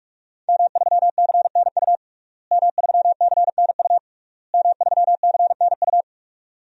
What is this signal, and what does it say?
Morse code, heard as a keyed tone, about 46 wpm, 700 Hz. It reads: M3CNU M3CNU M3CNU
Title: DXCC-100 Call Signs - 5 Characters - Repeated 3x - 45wpm